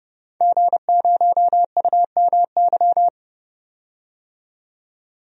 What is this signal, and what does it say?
Morse code, heard as a keyed tone, about 30 wpm, 700 Hz. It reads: G0UMY
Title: DXCC-100 Call Signs - 5 Characters - 30wpm